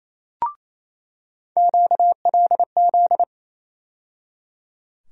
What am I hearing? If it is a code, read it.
Morse code, heard as a keyed tone, about 28 wpm, 700 Hz. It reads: QLZ
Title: USA Call Sign Postfixes - 3 Letters 28wpm